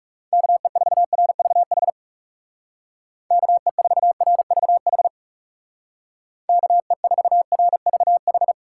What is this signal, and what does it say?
Morse code, heard as a keyed tone, about 36 wpm, 700 Hz. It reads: KE4RVH KE4RVH KE4RVH
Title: Speed Racing - USA 2x3 Call Signs 30wpm